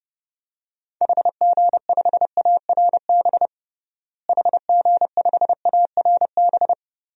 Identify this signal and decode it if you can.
Morse code, heard as a keyed tone, about 30 wpm, 700 Hz. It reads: HG5ARB HG5ARB
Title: DXCC-100 Call Signs - 6 Characters - Repeated 2x - 30wpm